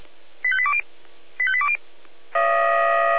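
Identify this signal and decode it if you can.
Five-tone selective call (ZVEI), heard as quick runs of five tones, tones of about 70 ms. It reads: 85820 85820
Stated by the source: ZVEI five-tone selective call received on SDR.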